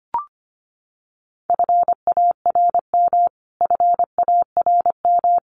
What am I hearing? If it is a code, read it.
Morse code, heard as a keyed tone, about 25 wpm, 700 Hz. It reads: FARM FARM
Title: Top 500 Words Repeated 25wpm